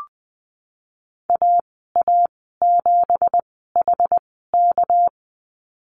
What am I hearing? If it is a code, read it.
Morse code, heard as a keyed tone, about 20 wpm, spaced as for 15 wpm, 700 Hz. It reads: AA7HK
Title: 219 Lesson 35g - Number 0 - Call Signs 20wpm